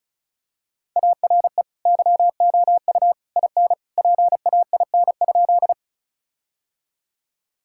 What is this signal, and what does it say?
Morse code, heard as a keyed tone, about 36 wpm, 700 Hz. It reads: ARE YOU IN PAIN?
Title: Sentences from Top 1600 Words 35wpm